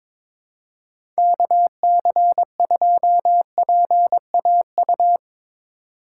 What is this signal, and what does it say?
Morse code, heard as a keyed tone, about 22 wpm, 700 Hz. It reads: KC2PAU
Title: USA Call Signs - 6 Characters - 2x Character Spacing - 22wpm